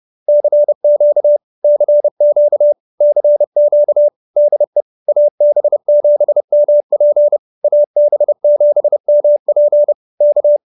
Morse code, heard as a keyed tone, about 30 wpm, 590 Hz. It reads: CQ CQ CQ DE AB7MP AB7MP K